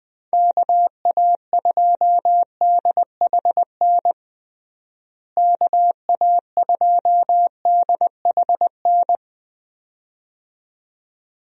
Morse code, heard as a keyed tone, about 20 wpm, 700 Hz. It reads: KA2DHN KA2DHN
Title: USA Call Signs - 6 Characters - Repeated 2x - 20wpm